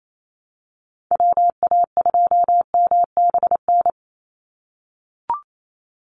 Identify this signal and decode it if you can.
Morse code, heard as a keyed tone, about 28 wpm, 700 Hz. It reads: WA2MBN
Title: USA Call Signs 28wpm